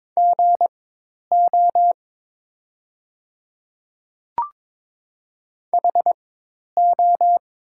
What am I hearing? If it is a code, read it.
Morse code, heard as a keyed tone, about 22 wpm, spaced as for 10 wpm, 700 Hz. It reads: GO HO
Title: 4x Character Spacing - Top 200 Words - 22wpm